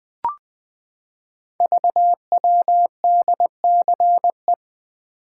Morse code, heard as a keyed tone, about 20 wpm, 700 Hz. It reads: VWDCE
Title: Code Groups - 5 Letters - 20wpm